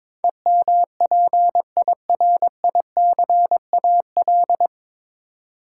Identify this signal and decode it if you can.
Morse code, heard as a keyed tone, about 22 wpm, 700 Hz. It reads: EMPIRICAL
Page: Morse Code Ninja practice 22wpm